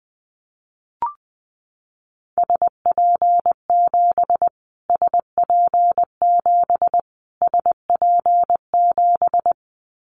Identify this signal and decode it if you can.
Morse code, heard as a keyed tone, about 20 wpm, 700 Hz. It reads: SP7 SP7 SP7
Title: Callsign Prefixes for Top 25 DXCC - Repeated 3x - 20wpm